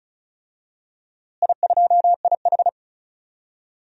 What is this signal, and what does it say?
Morse code, heard as a keyed tone, about 36 wpm, 700 Hz. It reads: I2IH
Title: DXCC-100 Call Signs - 4 Characters - Repeated 2x - 35wpm